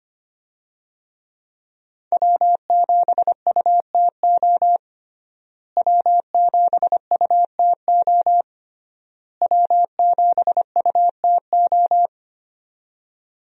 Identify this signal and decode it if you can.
Morse code, heard as a keyed tone, about 25 wpm, 700 Hz. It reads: W7UTO W7UTO W7UTO
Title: USA Call Signs - 5 Characters - Repeated 3x - 25wpm